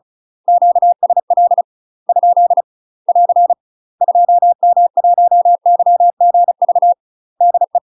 Morse code, heard as a keyed tone, about 34 wpm, 700 Hz. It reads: QSL ? <AR> 2M1YGV DE
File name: UK MASTER QSOs TRAINING/35WPM